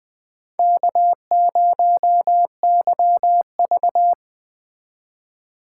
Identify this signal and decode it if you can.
Morse code, heard as a keyed tone, about 21 wpm, 700 Hz. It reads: K0YV
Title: Speed Racing - USA 1x2 Call Signs 17wpm